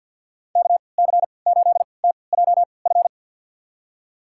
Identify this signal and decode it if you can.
Morse code, heard as a keyed tone, about 53 wpm, spaced as for 29 wpm, 700 Hz. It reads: KC8TJF
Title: USA Call Signs - 6 Characters - 3x Character Spacing - 50wpm